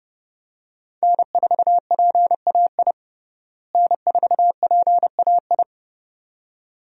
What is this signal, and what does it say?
Morse code, heard as a keyed tone, about 31 wpm, 700 Hz. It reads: N4PAI N4PAI
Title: USA Call Signs - 5 Characters - Repeated 2x - 30wpm